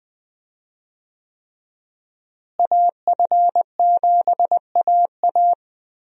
Morse code, heard as a keyed tone, about 20 wpm, 700 Hz. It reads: AF7AA
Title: USA Call Signs - 5 Characters - Repeated 3x - 20wpm